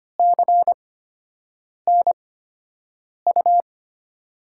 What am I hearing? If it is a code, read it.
Morse code, heard as a keyed tone, about 25 wpm, spaced as for 7 wpm, 700 Hz. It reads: CNU
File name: Top 90 QSO Elements - 8x Character Spacing - 25wpm